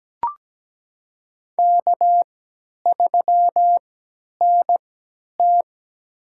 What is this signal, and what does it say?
Morse code, heard as a keyed tone, about 17 wpm, spaced as for 10 wpm, 700 Hz. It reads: K3NT
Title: USA Call Signs - 4 Characters - 3x Character Spacing - 17wpm